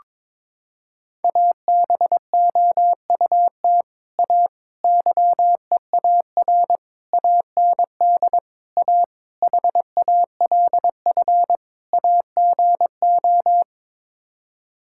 Morse code, heard as a keyed tone, about 21 wpm, 700 Hz. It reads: ABOUT A YEAR AND A HALF AGO
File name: practice / 22wpm / Sets of 7 Words - 22wpm